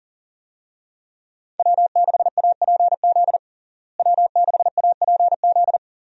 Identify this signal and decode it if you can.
Morse code, heard as a keyed tone, about 40 wpm, 700 Hz. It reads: W6APZ W6APZ
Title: USA Call Signs - 5 Characters - Repeated 3x - 40wpm